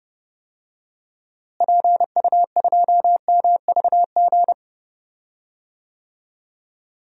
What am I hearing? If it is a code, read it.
Morse code, heard as a keyed tone, about 30 wpm, 700 Hz. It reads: PU2MVG
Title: DXCC-100 Call Signs - 6 Characters - Repeated 2x - 30wpm